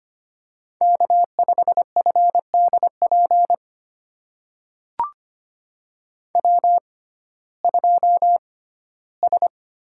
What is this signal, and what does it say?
Morse code, heard as a keyed tone, about 25 wpm, spaced as for 15 wpm, 700 Hz. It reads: K5FDP W2S
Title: USA Call Signs - 5 Characters - 6x Character Spacing - 25wpm